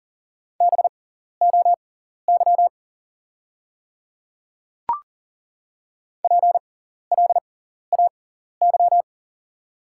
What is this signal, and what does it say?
Morse code, heard as a keyed tone, about 40 wpm, spaced as for 14 wpm, 690 Hz. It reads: BOY PLAY
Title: 6x Character Spacing - Top 500 Words - 40wpm